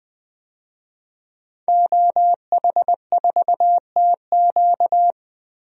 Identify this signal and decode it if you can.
Morse code, heard as a keyed tone, about 20 wpm, 700 Hz. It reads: OH4TQ
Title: DXCC-100 Call Signs - 5 Characters - 20wpm